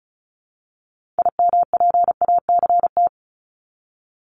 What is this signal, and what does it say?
Morse code, heard as a keyed tone, about 35 wpm, 700 Hz. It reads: IMPACT
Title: Top 751-1000 Words 35wpm